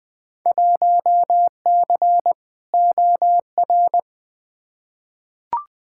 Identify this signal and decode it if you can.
Morse code, heard as a keyed tone, about 21 wpm, 700 Hz. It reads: 1C OR
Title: Field Day Exchanges - 20wpm